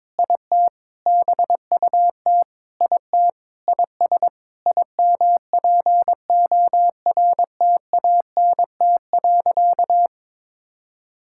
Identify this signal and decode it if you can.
Morse code, heard as a keyed tone, about 22 wpm, 700 Hz. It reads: IT BUT IT IS IMPORTANT.
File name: practice / 22wpm / Sentences with 8-Word Phrases - 22wpm